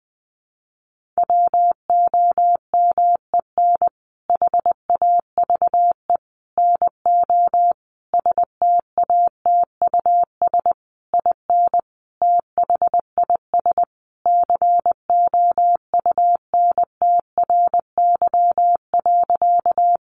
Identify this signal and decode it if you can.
Morse code, heard as a keyed tone, about 20 wpm, 700 Hz. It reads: WOMEN HAVE NO STATUS IN THIS COUNTRY.
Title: Sentences from Top 1400 Words 20wpm